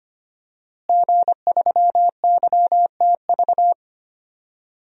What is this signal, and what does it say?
Morse code, heard as a keyed tone, about 25 wpm, 700 Hz. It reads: G3YTV
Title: DXCC-100 Call Signs - 5 Characters - Repeated 3x - 25wpm